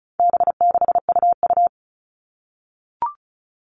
Morse code, heard as a keyed tone, about 34 wpm, 700 Hz. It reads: B6UU
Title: All Cut Numbers - 4-Digit 35wpm